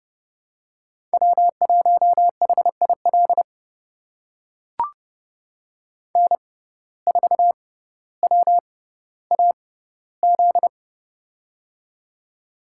Morse code, heard as a keyed tone, about 32 wpm, spaced as for 16 wpm, 700 Hz. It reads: W1HIL N4WAZ
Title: USA Call Signs - 5 Characters - 6x Character Spacing - 30wpm